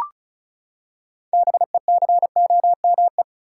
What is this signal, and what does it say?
Morse code, heard as a keyed tone, about 34 wpm, 700 Hz. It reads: BECOME
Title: Six Letter Words - 35wpm